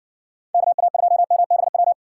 Morse code, heard as a keyed tone, about 59 wpm, 700 Hz. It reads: KA2MBW